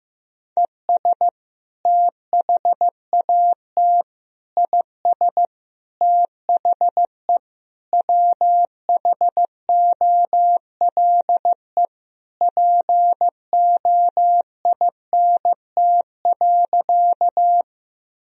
Morse code, heard as a keyed tone, about 15 wpm, 700 Hz. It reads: ES THAT IS THE WHOLE POINT.
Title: Sentences from Top 400 Words and Abbreviations 15wpm